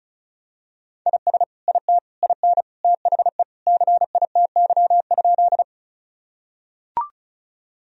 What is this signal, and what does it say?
Morse code, heard as a keyed tone, about 34 wpm, 700 Hz. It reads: IS IT IN THE CITY?
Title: Sentences from Top 1000 Words - Review Part 1 - 35wpm